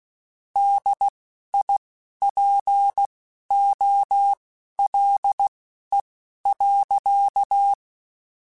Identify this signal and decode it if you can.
Morse code, heard as a keyed tone, about 16 wpm, spaced as for 11 wpm, 800 Hz. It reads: DIPOLE.
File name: QSOs / QSOs-mp3s / WD0FBT 1@10